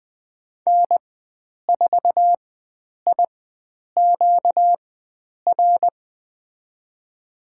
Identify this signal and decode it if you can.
Morse code, heard as a keyed tone, about 20 wpm, spaced as for 9 wpm, 700 Hz. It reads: N4IQR
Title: USA Call Signs - 5 Characters - 4x Character Spacing - 20wpm